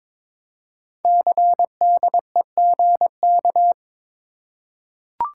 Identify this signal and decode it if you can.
Morse code, heard as a keyed tone, about 22 wpm, 700 Hz. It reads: CDEGK
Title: Code Groups - 5 Letters - 22wpm